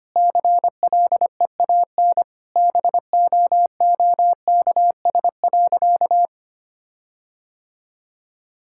Morse code, heard as a keyed tone, about 25 wpm, 700 Hz. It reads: CLEAN BOOKS.
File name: practice / 25wpm / Sentences with 4-Word Phrases - 25wpm